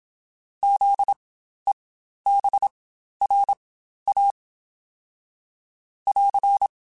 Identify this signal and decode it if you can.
Morse code, heard as a keyed tone, about 27 wpm, spaced as for 12 wpm, 800 Hz. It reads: ZEBRA <AR>
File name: Spaced Alphas / Spaced Alphas mp3s / Zs sp @25